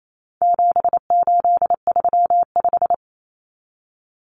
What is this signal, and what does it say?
Morse code, heard as a keyed tone, about 28 wpm, 700 Hz. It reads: 7835
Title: Numbers - 4-Digit 28wpm